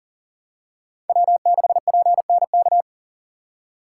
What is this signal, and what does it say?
Morse code, heard as a keyed tone, about 40 wpm, 700 Hz. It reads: W6PNK
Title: USA Call Signs - 5 Characters - 6x Character Spacing - 40wpm